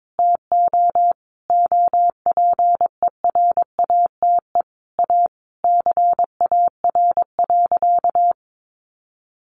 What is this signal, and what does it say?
Morse code, heard as a keyed tone, about 22 wpm, 700 Hz. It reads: TO OPERATE A CAR.
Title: Sentences from Top 1500 Words 22wpm